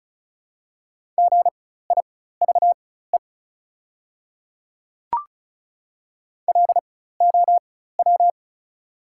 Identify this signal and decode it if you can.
Morse code, heard as a keyed tone, about 36 wpm, spaced as for 16 wpm, 700 Hz. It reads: GIVE LOW
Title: 4x Character Spacing - Top 200 Words - 35wpm